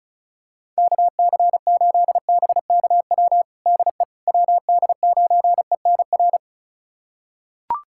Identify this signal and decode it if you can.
Morse code, heard as a keyed tone, about 34 wpm, 690 Hz. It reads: KC8BKW DE WD9ENR
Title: USA Callsign DE Callsign - 6 Characters - 35wpm